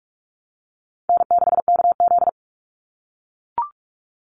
Morse code, heard as a keyed tone, about 44 wpm, 700 Hz. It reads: N6KZ
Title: NA SOTA Chaser Callsigns 45wpm